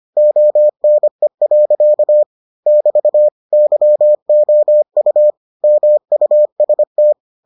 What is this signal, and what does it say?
Morse code, heard as a keyed tone, about 25 wpm, 600 Hz. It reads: ONE. <BT> YOU MUST